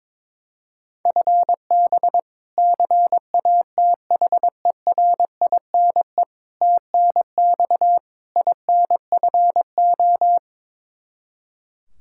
Morse code, heard as a keyed tone, about 22 wpm, 700 Hz. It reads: FB CATHERINE TNX INFO
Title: K1USN SST QSO Practice 22wpm